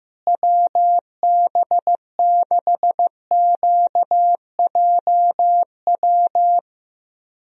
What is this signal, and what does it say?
Morse code, heard as a keyed tone, about 15 wpm, 700 Hz. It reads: WB6QJW
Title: USA Call Signs - 6 Characters - Repeated 3x - 15wpm